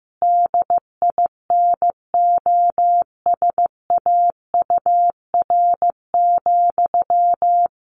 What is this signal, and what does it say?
Morse code, heard as a keyed tone, about 15 wpm, 700 Hz. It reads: DINOSAUR,
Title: A Sound Of Thunder 15wpm